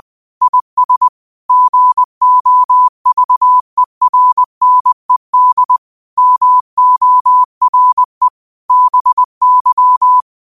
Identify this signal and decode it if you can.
Morse code, heard as a keyed tone, about 20 wpm, 1000 Hz. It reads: IS GOVERNED MORE BY
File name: qotdfiles / 20wpm